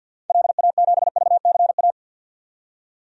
Morse code, heard as a keyed tone, about 51 wpm, 700 Hz. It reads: LA7VCA
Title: DXCC-100 Call Signs - 6 Characters - 50wpm